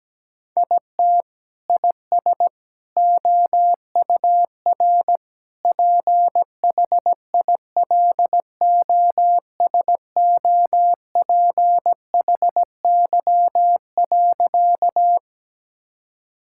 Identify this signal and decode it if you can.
Morse code, heard as a keyed tone, about 17 wpm, 700 Hz. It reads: IT IS OUR PHILOSOPHY.
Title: Sentences from Top 1000 Words - Review Part 6 - 17wpm